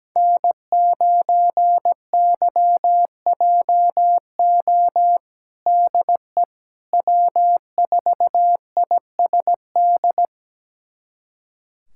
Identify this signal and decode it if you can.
Morse code, heard as a keyed tone, about 17 wpm, 700 Hz. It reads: N9YJO DE W4ISD
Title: USA Callsign DE Callsign - 5 Characters - 17wpm